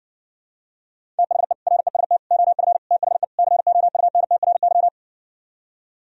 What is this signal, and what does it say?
Morse code, heard as a keyed tone, about 70 wpm, 700 Hz. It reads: THE REST OF THE COUNTRY